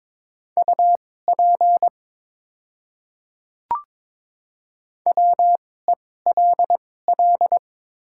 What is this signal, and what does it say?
Morse code, heard as a keyed tone, about 22 wpm, spaced as for 16 wpm, 700 Hz. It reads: UP WELL
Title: 2x Character Spacing - Top 200 Words - 22wpm